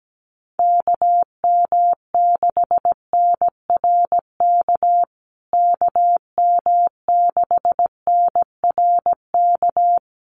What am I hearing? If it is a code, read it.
Morse code, heard as a keyed tone, about 17 wpm, 700 Hz. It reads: KM6NRK KM6NRK
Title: USA Call Signs Repeated 2x 17wpm